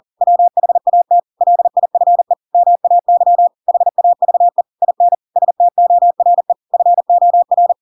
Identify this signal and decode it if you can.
Morse code, heard as a keyed tone, about 40 wpm, 700 Hz. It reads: WHAT LIFE MAY HAVE IN STORE FOR
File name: Pinocchio / Pinocchio ch7 40 40 ik1hge